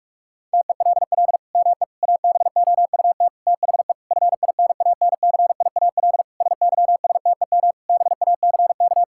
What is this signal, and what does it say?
Morse code, heard as a keyed tone, about 41 wpm, 700 Hz. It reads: TELL ME ABOUT THE FINANCIAL SYSTEM BACK
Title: Sentences from Top 1000 Words - Review Part 5 - 45wpm